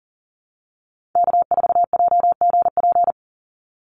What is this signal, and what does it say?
Morse code, heard as a keyed tone, about 40 wpm, 700 Hz. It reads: K4JGP
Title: USA Call Signs 40wpm